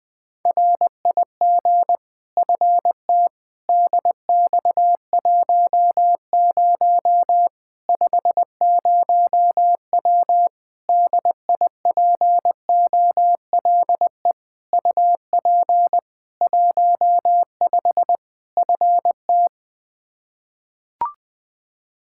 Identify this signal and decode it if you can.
Morse code, heard as a keyed tone, about 20 wpm, 700 Hz. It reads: RIG FT DX10 50W DIPOLE UP 15 FT
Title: K1USN SST QSO Practice 20wpm